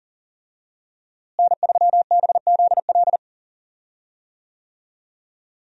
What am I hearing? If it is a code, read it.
Morse code, heard as a keyed tone, about 40 wpm, 700 Hz. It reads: N3BZL
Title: USA Call Signs - 5 Characters - Repeated 3x - 40wpm